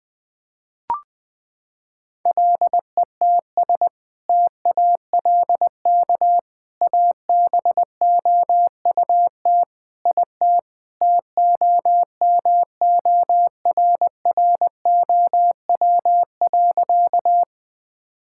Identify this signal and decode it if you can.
Morse code, heard as a keyed tone, about 19 wpm, 700 Hz. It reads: LETS TALK ABOUT IT TOMORROW.
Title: Sentences with 2-Word Phrases - 20wpm